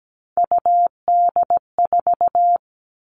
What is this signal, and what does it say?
Morse code, heard as a keyed tone, about 17 wpm, 700 Hz. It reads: UD4
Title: All Cut Numbers - 3-Digit 17wpm